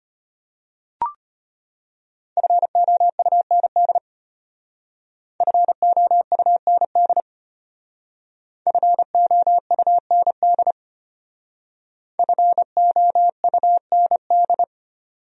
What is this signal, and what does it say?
Morse code, heard as a keyed tone, about 33 wpm, 700 Hz. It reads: FOUND FOUND FOUND FOUND